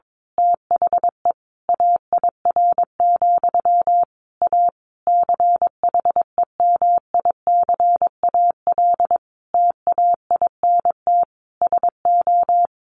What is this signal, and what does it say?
Morse code, heard as a keyed tone, about 22 wpm, 700 Hz. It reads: THE AIR, A CHEMICAL TAINT SO
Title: A Sound Of Thunder 22wpm